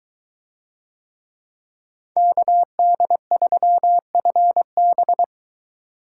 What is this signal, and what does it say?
Morse code, heard as a keyed tone, about 23 wpm, 700 Hz. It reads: KD3FB